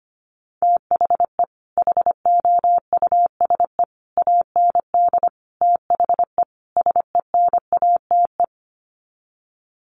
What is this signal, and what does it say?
Morse code, heard as a keyed tone, about 25 wpm, 700 Hz. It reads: THE HOUSE AND THE SENATE
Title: Sets of 5 Words Encore2 25wpm